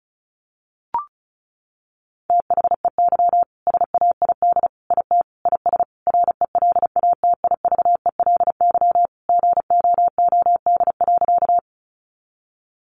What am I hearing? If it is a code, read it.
Morse code, heard as a keyed tone, about 34 wpm, 700 Hz. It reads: THEY SAID IT IS RELATIVELY GOOD.
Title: Sentences from Top 1100 Words 35wpm